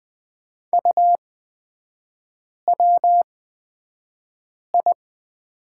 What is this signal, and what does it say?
Morse code, heard as a keyed tone, about 20 wpm, spaced as for 5 wpm, 700 Hz. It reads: UWI
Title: 255 Lesson 41e - Period - Three Characters 20wpm